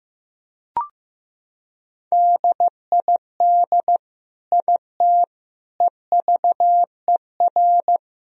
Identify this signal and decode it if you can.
Morse code, heard as a keyed tone, about 15 wpm, 700 Hz. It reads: DID IT EVER
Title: Sentences from Top 1000 Words - Review Part 2 - 15wpm